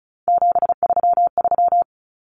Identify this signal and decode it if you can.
Morse code, heard as a keyed tone, about 34 wpm, 700 Hz. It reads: 733
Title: Common Cut Numbers - 3-Digit 35wpm